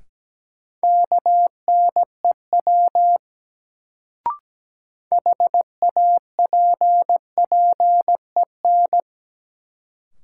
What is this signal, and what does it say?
Morse code, heard as a keyed tone, about 17 wpm, 700 Hz. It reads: KNEW HAPPEN
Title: Sentences from Top 400 Words - New Word Drill - 17wpm